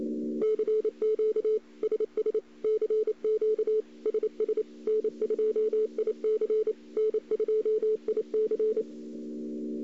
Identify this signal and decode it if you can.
Morse code, heard as a keyed tone, about 28 wpm, 430 Hz. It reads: CQ SS CQ SS N2IC N2IC
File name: CW / 2008-ss-cw-k5tr-op-n5rz -02:00